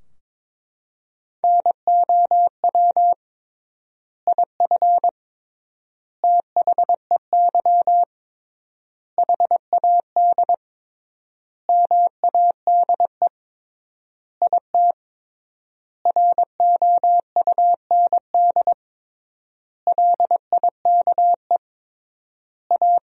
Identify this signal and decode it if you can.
Morse code, heard as a keyed tone, about 22 wpm, 700 Hz. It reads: NOW IF THEY HAD MADE IT ROUND LIKE A
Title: 3x Word-Spacing - Sentences from Top 500 Words - 22wpm